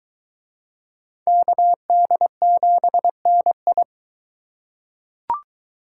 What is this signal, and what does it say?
Morse code, heard as a keyed tone, about 20 wpm, 700 Hz. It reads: KD7NI